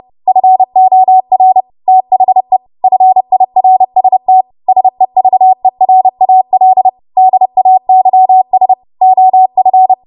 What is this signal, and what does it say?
Morse code, heard as a keyed tone, about 30 wpm, 750 Hz. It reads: FOR THE FIRST SEVERAL DAYS OF
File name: Morse / 30WPM